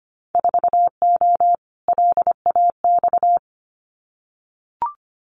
Morse code, heard as a keyed tone, about 25 wpm, 700 Hz. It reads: 4O LAX
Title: Winter Field Day Exchanges Encore 25wpm